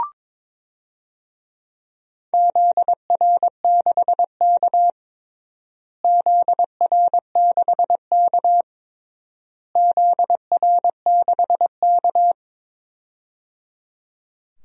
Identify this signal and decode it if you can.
Morse code, heard as a keyed tone, about 22 wpm, 700 Hz. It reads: ZR6K ZR6K ZR6K
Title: DXCC-100 Call Signs - 4 Characters - Repeated 3x - 22wpm